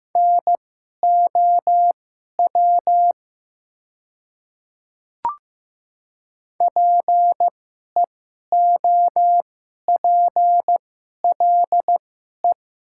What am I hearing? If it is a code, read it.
Morse code, heard as a keyed tone, about 15 wpm, spaced as for 11 wpm, 700 Hz. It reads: NOW PEOPLE
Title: 2x Character Spacing - Top 100 Words - 15wpm